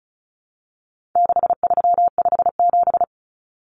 Morse code, heard as a keyed tone, about 34 wpm, 700 Hz. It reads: 6357
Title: Common Cut Numbers - 4-Digit 35wpm